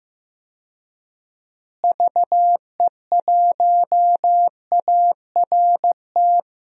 Morse code, heard as a keyed tone, about 15 wpm, 700 Hz. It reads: VE1ART
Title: DXCC-100 Call Signs - 6 Characters - Repeated 3x - 15wpm